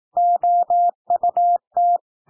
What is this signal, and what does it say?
Morse code, heard as a keyed tone, about 18 wpm, 700 Hz. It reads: OUT
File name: TEXT PASSAGE 18 wpm